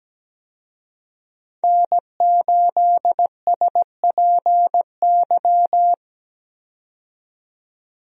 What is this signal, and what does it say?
Morse code, heard as a keyed tone, about 17 wpm, 700 Hz. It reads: N8SPY
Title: USA Call Signs - 5 Characters - 17wpm